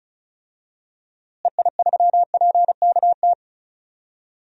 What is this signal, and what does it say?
Morse code, heard as a keyed tone, about 36 wpm, 700 Hz. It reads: EI3PKT
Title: DXCC-100 Call Signs - 6 Characters - Repeated 2x - 35wpm